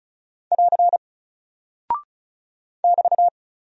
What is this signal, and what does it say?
Morse code, heard as a keyed tone, about 34 wpm, 700 Hz. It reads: <AR> <BT>
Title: Single Character (Letter:Number:Punctuation:Prosign) 35wpm